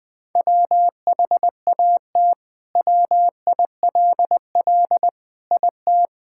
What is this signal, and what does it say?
Morse code, heard as a keyed tone, about 20 wpm, 700 Hz. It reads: WHAT WILL IT
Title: Sentences from Top 1600 Words 20wpm